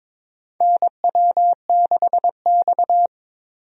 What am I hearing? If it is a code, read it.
Morse code, heard as a keyed tone, about 22 wpm, 700 Hz. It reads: NW6X
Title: USA Call Signs - 4 Characters - Repeated 2x - 22wpm